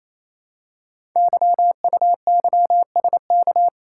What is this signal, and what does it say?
Morse code, heard as a keyed tone, about 28 wpm, 700 Hz. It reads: YUYSK
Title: Code Groups - 5 Letters - 28wpm